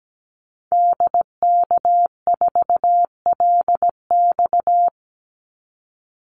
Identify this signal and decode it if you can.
Morse code, heard as a keyed tone, about 17 wpm, 700 Hz. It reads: DK4LX
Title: Call Signs - 17wpm